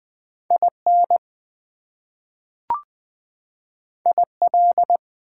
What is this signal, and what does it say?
Morse code, heard as a keyed tone, about 20 wpm, 700 Hz. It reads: IN IL
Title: US States and Canadian Provinces 20wpm